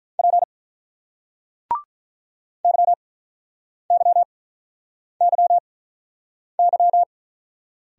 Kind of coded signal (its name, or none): Morse code